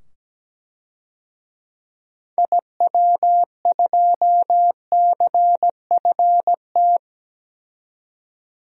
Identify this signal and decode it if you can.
Morse code, heard as a keyed tone, about 17 wpm, 700 Hz. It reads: IW2CFT
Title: DXCC-100 Call Signs - 6 Characters - 17wpm